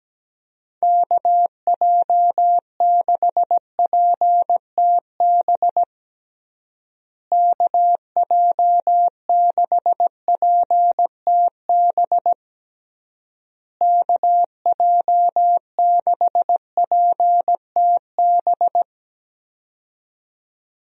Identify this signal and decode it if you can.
Morse code, heard as a keyed tone, about 17 wpm, 700 Hz. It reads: KJ6PTB KJ6PTB KJ6PTB
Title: USA Call Signs - 6 Characters - Repeated 3x - 17wpm